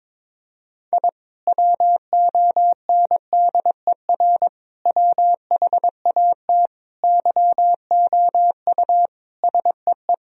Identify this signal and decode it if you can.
Morse code, heard as a keyed tone, about 22 wpm, 700 Hz. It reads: I WONDER WHAT YOU SEE
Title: Sentences from Top 1000 Words - Review Part 3 - 22wpm